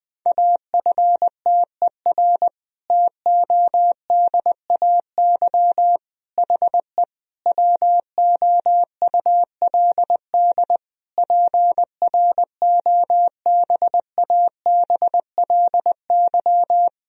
Morse code, heard as a keyed tone, about 20 wpm, 700 Hz. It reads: AFTER TODAY HE WOULD PROBABLY
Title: Sentences from Top 1000 Words - Review Part 4 - 20wpm